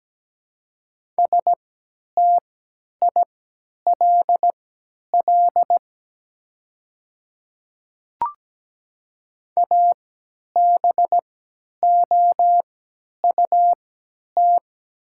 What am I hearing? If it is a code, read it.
Morse code, heard as a keyed tone, about 17 wpm, spaced as for 10 wpm, 700 Hz. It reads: STILL ABOUT
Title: 3x Character Spacing - Top 300 Words - 17wpm